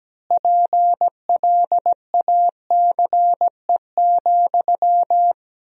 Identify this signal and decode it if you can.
Morse code, heard as a keyed tone, about 17 wpm, 700 Hz. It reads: PLACE,
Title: 3x Word-Spacing - Sentences from Top 400 Words - 17wpm